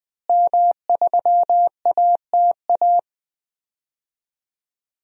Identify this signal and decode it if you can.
Morse code, heard as a keyed tone, about 20 wpm, 700 Hz. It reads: M3ATA
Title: DXCC-100 Call Signs - 5 Characters - Repeated 3x - 20wpm